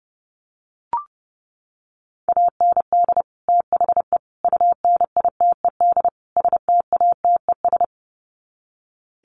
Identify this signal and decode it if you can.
Morse code, heard as a keyed tone, about 30 wpm, 700 Hz. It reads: AND THE UNITED STATES
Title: Sets of 4 Words 30wpm